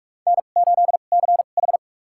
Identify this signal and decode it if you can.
Morse code, heard as a keyed tone, about 46 wpm, spaced as for 32 wpm, 700 Hz. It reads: N8CH
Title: USA Call Signs - 4 Characters - 2x Character Spacing - 45wpm